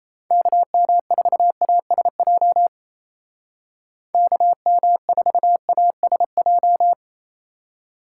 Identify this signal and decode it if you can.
Morse code, heard as a keyed tone, about 34 wpm, 700 Hz. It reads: KM4ASJ KM4ASJ
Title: Speed Racing - USA 2x3 Call Signs 28wpm